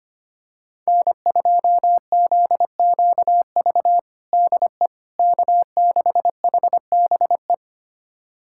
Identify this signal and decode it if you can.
Morse code, heard as a keyed tone, about 25 wpm, 700 Hz. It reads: N2ZQV DE K6HBE
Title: USA Callsign DE Callsign 25wpm